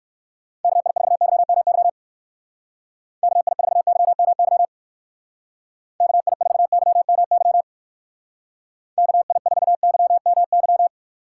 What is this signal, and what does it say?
Morse code, heard as a keyed tone, about 59 wpm, 700 Hz. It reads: KI4YMY KI4YMY KI4YMY KI4YMY